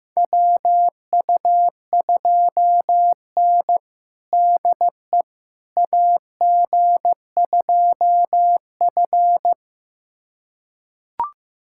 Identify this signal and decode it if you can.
Morse code, heard as a keyed tone, about 15 wpm, 700 Hz. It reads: WU2N DE AG2F
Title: USA Callsign DE Callsign - 4 Characters - 15wpm